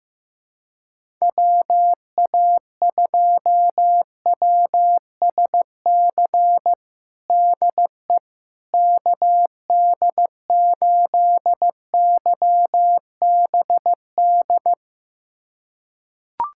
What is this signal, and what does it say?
Morse code, heard as a keyed tone, about 16 wpm, 700 Hz. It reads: WA2WSC DE KD8YBD
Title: USA Callsign DE Callsign - 6 Characters - 15wpm